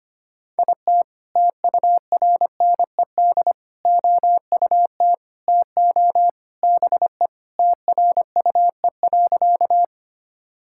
Morse code, heard as a keyed tone, about 25 wpm, 700 Hz. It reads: IT TURNED OUT TO BE TRUE.
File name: practice / 25wpm / Sentences with 3-Word Phrases - 25wpm